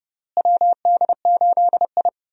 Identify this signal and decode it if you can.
Morse code, heard as a keyed tone, about 32 wpm, 700 Hz. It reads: WD8I
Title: USA Call Signs - 4 Characters - Repeated 2x - 30wpm